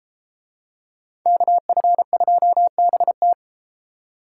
Morse code, heard as a keyed tone, about 32 wpm, 700 Hz. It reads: KF2BT